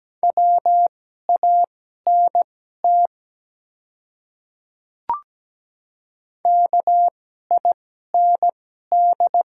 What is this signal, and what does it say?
Morse code, heard as a keyed tone, about 17 wpm, spaced as for 12 wpm, 700 Hz. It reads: WANT KIND
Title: 2x Character Spacing - Top 300 Words - 17wpm